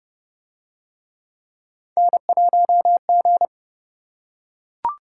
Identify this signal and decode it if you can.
Morse code, heard as a keyed tone, about 28 wpm, 700 Hz. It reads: N1G